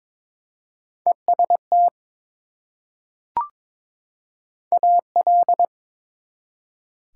Morse code, heard as a keyed tone, about 22 wpm, 700 Hz. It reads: EST AL
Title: Prefixes and Suffixes 22wpm